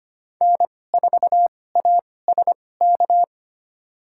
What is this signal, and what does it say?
Morse code, heard as a keyed tone, about 25 wpm, spaced as for 18 wpm, 700 Hz. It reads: N4ASK
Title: USA Call Signs - 5 Characters - 2x Character Spacing - 25wpm